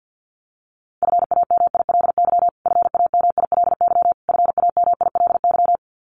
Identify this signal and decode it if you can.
Morse code, heard as a keyed tone, about 50 wpm, 700 Hz. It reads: FAMILY FAMILY FAMILY
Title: Top 500 Words Repeated 50wpm